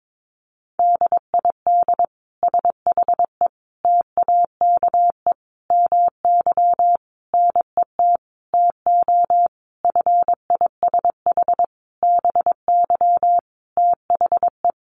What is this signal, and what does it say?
Morse code, heard as a keyed tone, about 22 wpm, 700 Hz. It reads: DID SHE TAKE MY NET TO FISH BY THE
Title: Sentences from Top 1400 Words 22wpm